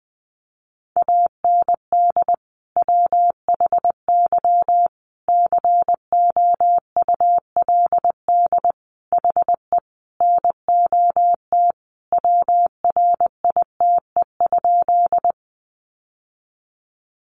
Morse code, heard as a keyed tone, about 20 wpm, 700 Hz. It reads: AND WHY COULD HE NOT WRITE?
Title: Sentences from Top 600 Words 20wpm